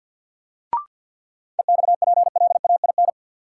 Morse code, heard as a keyed tone, about 49 wpm, 700 Hz. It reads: EXPLAIN
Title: Sentences from Top 1100 Words - New Word Drill - 50wpm